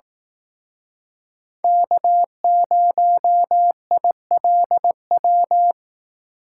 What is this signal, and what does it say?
Morse code, heard as a keyed tone, about 18 wpm, 700 Hz. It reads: K0ILW
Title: Speed Racing - USA Call Signs 15wpm